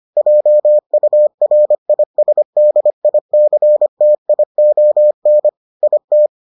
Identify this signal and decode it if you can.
Morse code, heard as a keyed tone, about 25 wpm, 600 Hz. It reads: JURISDICTION IT